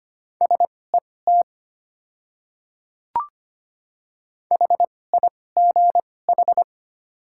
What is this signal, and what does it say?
Morse code, heard as a keyed tone, about 25 wpm, spaced as for 18 wpm, 690 Hz. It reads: SET HIGH
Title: 2x Character Spacing - Top 200 Words - 25wpm